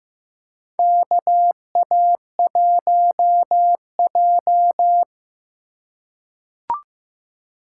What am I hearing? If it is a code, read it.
Morse code, heard as a keyed tone, about 15 wpm, 700 Hz. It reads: KA1J
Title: USA Call Signs - 4x Character Spacing - 15wpm